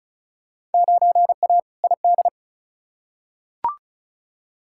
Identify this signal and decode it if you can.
Morse code, heard as a keyed tone, about 36 wpm, 700 Hz. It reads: 9A ID
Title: Field Day Exchanges Encore - 35wpm